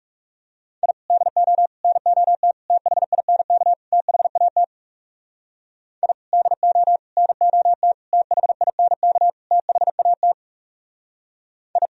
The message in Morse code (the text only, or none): I DO NOT THINK THAT I DO NOT THINK THAT I